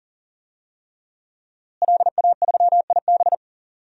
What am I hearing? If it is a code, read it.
Morse code, heard as a keyed tone, about 39 wpm, 700 Hz. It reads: LA3IB
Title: DXCC-100 Call Signs - 5 Characters - Repeated 3x - 40wpm